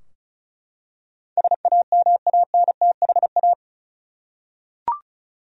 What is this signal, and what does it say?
Morse code, heard as a keyed tone, about 36 wpm, 700 Hz. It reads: SAMANTHA
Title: USA Common Names - 35wpm